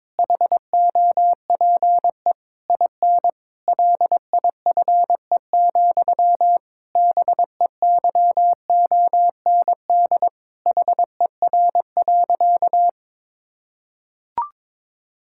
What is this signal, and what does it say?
Morse code, heard as a keyed tone, about 22 wpm, 700 Hz. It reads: HOPE IN LIFE, BEYOND HER.
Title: Sentences from Top 1000 Words - Review Part 6 - 22wpm